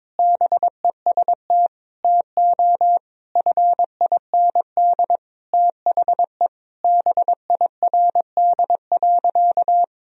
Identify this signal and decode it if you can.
Morse code, heard as a keyed tone, about 21 wpm, 700 Hz. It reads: BEST TO FIND THE BIRD.
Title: Sentences from Top 1000 Words - Review Part 2 - 22wpm